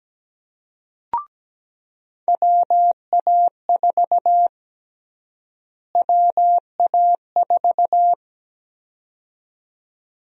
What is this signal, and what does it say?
Morse code, heard as a keyed tone, about 17 wpm, 700 Hz. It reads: WA4 WA4
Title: USA Call Sign Prefixes - Repeated 2x - 17wpm